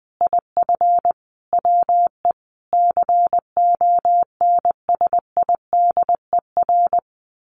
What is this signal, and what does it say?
Morse code, heard as a keyed tone, about 20 wpm, 700 Hz. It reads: IF WE CONSIDER
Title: Sentences from Top 1100 Words 20wpm